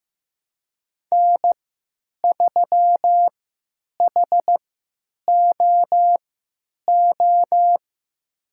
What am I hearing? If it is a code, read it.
Morse code, heard as a keyed tone, about 15 wpm, spaced as for 9 wpm, 700 Hz. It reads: N3HOO
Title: USA Call Signs - 5 Characters - 3x Character Spacing - 15wpm